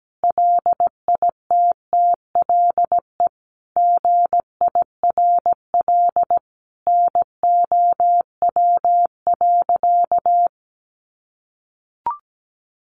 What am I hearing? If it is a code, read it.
Morse code, heard as a keyed tone, about 17 wpm, 700 Hz. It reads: LITTLE GIRL NOW.
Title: Sentences from Top 1100 Words 17wpm